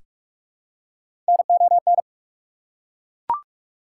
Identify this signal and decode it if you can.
Morse code, heard as a keyed tone, about 42 wpm, 690 Hz. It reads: NON
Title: Prefixes and Suffixes 45wpm